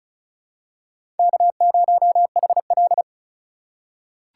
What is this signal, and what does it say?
Morse code, heard as a keyed tone, about 34 wpm, 700 Hz. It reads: K0HL
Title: USA Call Signs - 4 Characters - 2x Character Spacing - 35wpm